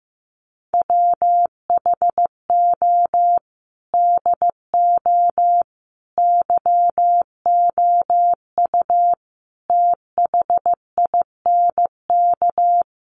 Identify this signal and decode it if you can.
Morse code, heard as a keyed tone, about 15 wpm, 700 Hz. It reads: WHO DO YOU THINK
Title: Sentences from Top 1500 Words 15wpm